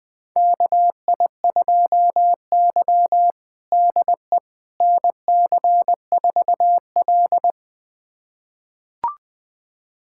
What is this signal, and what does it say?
Morse code, heard as a keyed tone, about 20 wpm, 700 Hz. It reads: KI2Y DE NC4L
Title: USA Callsign DE Callsign - 4 Characters - 20wpm